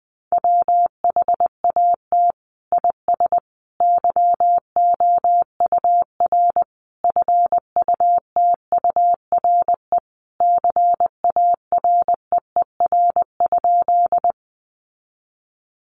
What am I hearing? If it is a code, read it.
Morse code, heard as a keyed tone, about 20 wpm, 700 Hz. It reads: WHAT IS YOUR FUTURE CAREER?
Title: Sentences from Top 1000 Words 20wpm